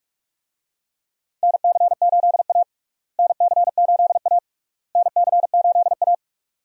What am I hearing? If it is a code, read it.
Morse code, heard as a keyed tone, about 47 wpm, 700 Hz. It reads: NC8A NC8A NC8A
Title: USA Call Signs - 4 Characters - Repeated 3x - 45wpm